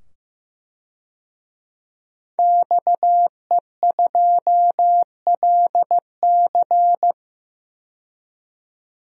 Morse code, heard as a keyed tone, about 15 wpm, 700 Hz. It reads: XE2LC
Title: DXCC-100 Call Signs - 5 Characters - 15wpm